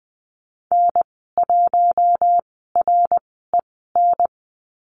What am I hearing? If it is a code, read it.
Morse code, heard as a keyed tone, about 20 wpm, spaced as for 14 wpm, 700 Hz. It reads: N1REN
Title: 062 Lesson 11g - Letter h - Call Signs 20wpm